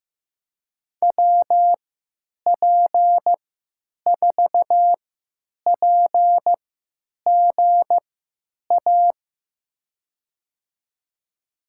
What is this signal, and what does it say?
Morse code, heard as a keyed tone, about 16 wpm, spaced as for 9 wpm, 700 Hz. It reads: WP4PGA
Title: USA Call Signs - 6 Characters - 3x Character Spacing - 15wpm